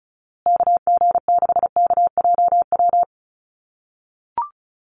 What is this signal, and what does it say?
Morse code, heard as a keyed tone, about 34 wpm, 700 Hz. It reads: KG6KJW
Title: USA Call Signs Repeated 2x 35wpm